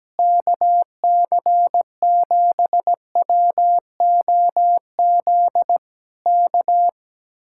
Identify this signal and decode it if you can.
Morse code, heard as a keyed tone, about 17 wpm, 700 Hz. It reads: KC7WOZ K
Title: Speed-Racing Common QSO Phrases - 17wpm